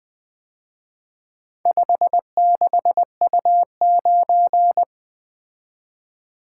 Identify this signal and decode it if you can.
Morse code, heard as a keyed tone, about 24 wpm, 700 Hz. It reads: 56U9
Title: Speed-Racing - Four Letter-Number Combinations - 20wpm